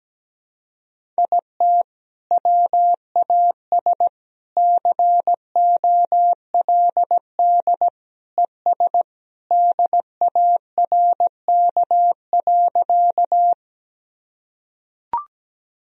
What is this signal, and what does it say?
Morse code, heard as a keyed tone, about 17 wpm, 700 Hz. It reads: IT WAS COLD ES DARK.
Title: Sentences from Top 500 Words and Abbreviations 17wpm